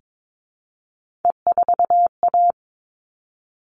Morse code, heard as a keyed tone, about 22 wpm, 700 Hz. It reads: E4A
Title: All Cut Numbers - 3-Digit 22wpm